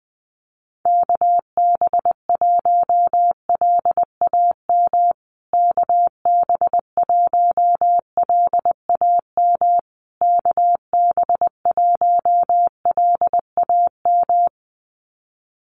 Morse code, heard as a keyed tone, about 20 wpm, 700 Hz. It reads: KB1LAM KB1LAM KB1LAM
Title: USA Call Signs Repeated 3x 20wpm